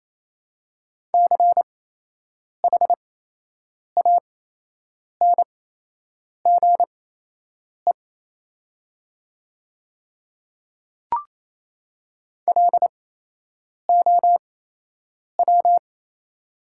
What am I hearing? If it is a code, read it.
Morse code, heard as a keyed tone, about 27 wpm, spaced as for 8 wpm, 700 Hz. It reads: CHANGE LOW
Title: 8x Character Spacing - Top 200 Words - 28wpm